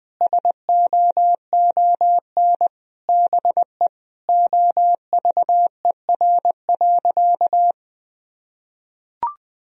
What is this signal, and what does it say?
Morse code, heard as a keyed tone, about 20 wpm, 700 Hz. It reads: SOON BE OVER.
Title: Sentences from Top 1000 Words - Review Part 2 - 20wpm